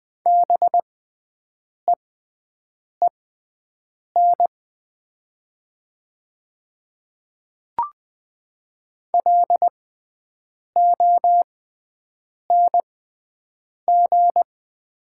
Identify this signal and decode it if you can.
Morse code, heard as a keyed tone, about 20 wpm, spaced as for 7 wpm, 700 Hz. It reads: BEEN LONG
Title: 6x Character Spacing - Top 200 Words - 20wpm